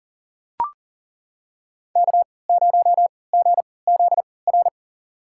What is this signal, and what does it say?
Morse code, heard as a keyed tone, about 40 wpm, spaced as for 23 wpm, 700 Hz. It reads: K0GZR
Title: USA Call Signs - 3x Character Spacing - 40wpm